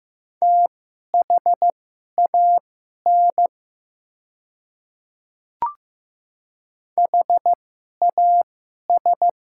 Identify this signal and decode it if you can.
Morse code, heard as a keyed tone, about 15 wpm, spaced as for 11 wpm, 700 Hz. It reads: THAN HAS
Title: 2x Character Spacing - Top 200 Words - 15wpm